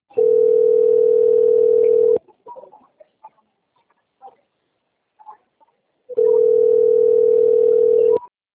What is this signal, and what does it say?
Ringback tone (North America), 440 + 480 Hz.
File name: out- - - - - .67603